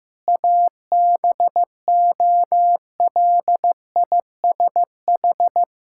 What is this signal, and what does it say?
Morse code, heard as a keyed tone, about 15 wpm, 700 Hz. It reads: ABOLISH
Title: Pangram Challenge v2 - 15wpm